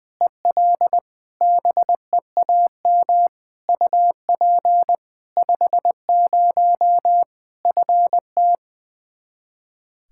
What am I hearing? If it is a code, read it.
Morse code, heard as a keyed tone, about 20 wpm, 700 Hz. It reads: EL BEAM UP 50 FT
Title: K1USN SST QSO Practice 20wpm